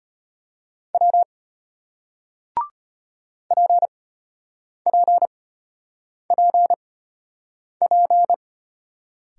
Morse code, heard as a keyed tone, about 33 wpm, spaced as for 8 wpm, 700 Hz. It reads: W PPPP